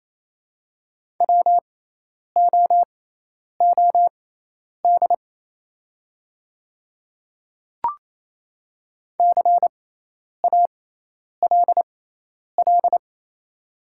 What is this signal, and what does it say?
Morse code, heard as a keyed tone, about 28 wpm, spaced as for 10 wpm, 690 Hz. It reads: WOOD CALL
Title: 6x Character Spacing - Top 300 Words - 28wpm